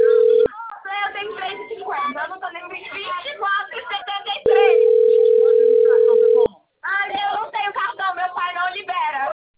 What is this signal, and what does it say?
Ringback tone (North America), 440 + 480 Hz, with speech.
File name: rg-1600-unknown- - - .37018